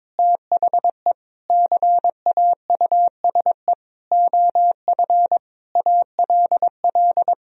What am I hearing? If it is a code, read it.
Morse code, heard as a keyed tone, about 22 wpm, 700 Hz. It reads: THE CAUSE OF ALL
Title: Sentences from Top 1000 Words - Review Part 1 - 22wpm